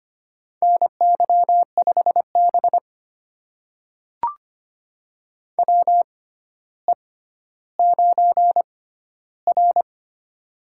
Morse code, heard as a keyed tone, about 25 wpm, spaced as for 13 wpm, 700 Hz. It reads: NY5B WE9R
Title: USA Call Signs - 4 Characters - 6x Character Spacing - 25wpm